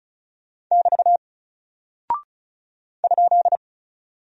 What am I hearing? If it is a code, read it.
Morse code, heard as a keyed tone, about 35 wpm, 700 Hz. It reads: <BT> ?
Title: Single Character (Letter:Number:Punctuation:Prosign) Alt 35wpm